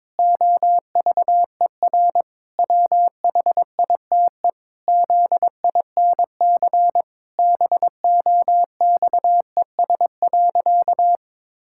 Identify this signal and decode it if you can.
Morse code, heard as a keyed tone, about 21 wpm, 700 Hz. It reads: OVER WHITE ZINC BOXES.
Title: Pangram Challenge v2 - 22wpm